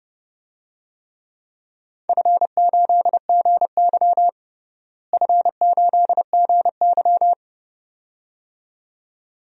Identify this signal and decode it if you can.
Morse code, heard as a keyed tone, about 32 wpm, 700 Hz. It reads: F8GY F8GY
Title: DXCC-100 Call Signs - 4 Characters - Repeated 2x - 30wpm